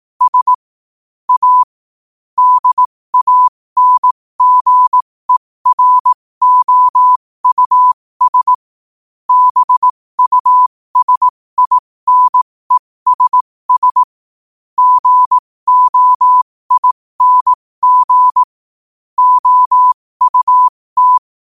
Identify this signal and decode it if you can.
Morse code, heard as a keyed tone, about 18 wpm, spaced as for 15 wpm, 1000 Hz. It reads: S A DANGEROUS BUSINESS GOING OUT
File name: qotdfiles / 15wpm